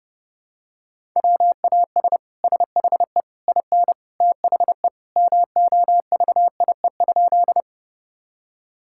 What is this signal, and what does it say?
Morse code, heard as a keyed tone, about 30 wpm, 700 Hz. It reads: WAS SHE IN THE MOVIE?
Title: Sentences from Top 1000 Words - Review Part 5 - 30wpm